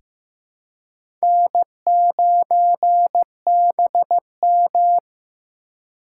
Morse code, heard as a keyed tone, about 15 wpm, 700 Hz. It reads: N9BM
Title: USA Call Signs - 4 Characters - 4x Character Spacing - 15wpm